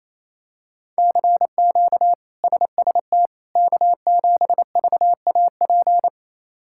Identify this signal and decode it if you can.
Morse code, heard as a keyed tone, about 28 wpm, 700 Hz. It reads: CQ SST K7VAP
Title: K1USN SST QSO Practice - 28wpm